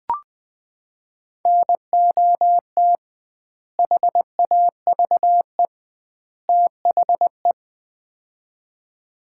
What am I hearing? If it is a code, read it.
Morse code, heard as a keyed tone, about 20 wpm, 700 Hz. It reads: NOT HAVE THE
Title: 2x Word-Spacing - Sets of 3 Words - 20wpm